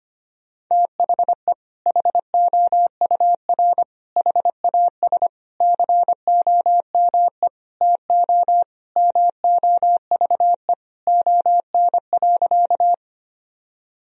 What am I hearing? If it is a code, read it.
Morse code, heard as a keyed tone, about 25 wpm, 700 Hz. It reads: THE HOUR HAS COME TO MOVE ON.
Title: Sentences from Top 1000 Words - Review Part 1 - 25wpm